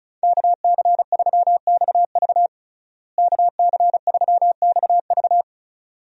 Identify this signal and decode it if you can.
Morse code, heard as a keyed tone, about 36 wpm, 700 Hz. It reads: KC3XV KC3XV
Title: USA Call Signs - 5 Characters - Repeated 2x - 35wpm